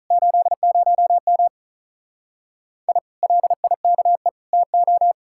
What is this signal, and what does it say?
Morse code, heard as a keyed tone, about 46 wpm, 700 Hz. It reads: 80M I LIKE TO